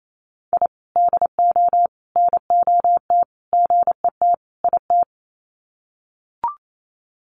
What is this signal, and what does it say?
Morse code, heard as a keyed tone, about 28 wpm, 700 Hz. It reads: I DO NOT GET IT
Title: Sets of 5 Words Encore 28wpm